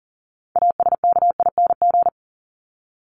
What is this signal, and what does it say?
Morse code, heard as a keyed tone, about 40 wpm, 700 Hz. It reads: ASKING
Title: ING Suffix - Top 100 Words with Base Words 40wpm